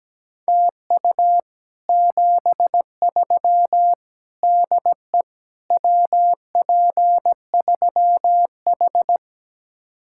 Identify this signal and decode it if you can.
Morse code, heard as a keyed tone, about 17 wpm, 700 Hz. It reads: TU 73 DE WP3H
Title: Speed-Racing Common QSO Phrases - 17wpm